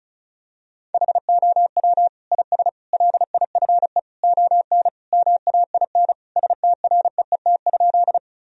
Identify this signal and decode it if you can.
Morse code, heard as a keyed tone, about 36 wpm, 700 Hz. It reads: HOW IS LIFE ON MAIN STREET?
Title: Sentences from Top 1000 Words - Review Part 2 - 35wpm